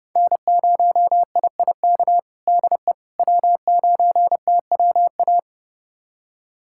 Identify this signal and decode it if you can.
Morse code, heard as a keyed tone, about 30 wpm, 700 Hz. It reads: N0IIK DE W9TWA
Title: USA Callsign DE Callsign - 5 Characters - 30wpm